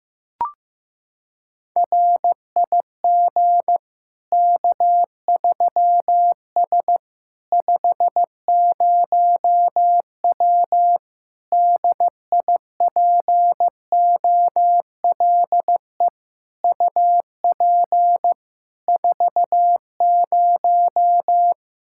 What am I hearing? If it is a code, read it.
Morse code, heard as a keyed tone, about 15 wpm, 700 Hz. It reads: RIG K3S 50W DIPOLE UP 40
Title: K1USN SST QSO Practice 15wpm